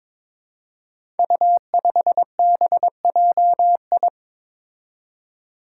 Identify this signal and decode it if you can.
Morse code, heard as a keyed tone, about 22 wpm, 700 Hz. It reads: U5BJI
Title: Five Challenging Letter-Number Combinations 22wpm